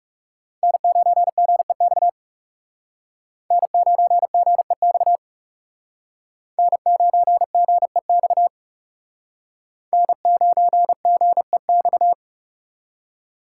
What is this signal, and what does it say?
Morse code, heard as a keyed tone, about 37 wpm, 700 Hz. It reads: N9GEX N9GEX N9GEX N9GEX
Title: Speed Racing - USA 1x3 Call Signs 30wpm